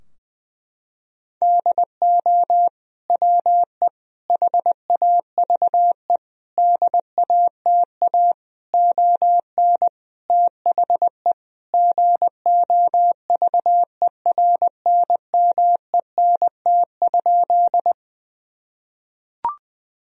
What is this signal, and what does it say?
Morse code, heard as a keyed tone, about 19 wpm, 700 Hz. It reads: DO WE HAVE DATA ON THE GOVERNMENT?
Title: Sentences from Top 1000 Words - Review Part 3 - 20wpm